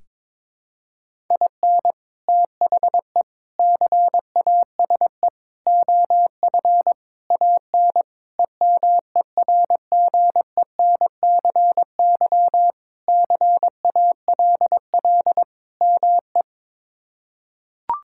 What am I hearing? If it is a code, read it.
Morse code, heard as a keyed tone, about 22 wpm, 700 Hz. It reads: IN THE CASE OF AN EMERGENCY CALL ME
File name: practice / 22wpm / Sentences with 3-Word Phrases - 22wpm